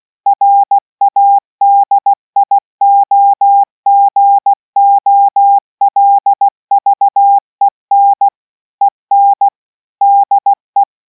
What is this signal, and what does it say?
Morse code, heard as a keyed tone, about 16 wpm, 800 Hz. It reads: RADIOGOLVEN EN DE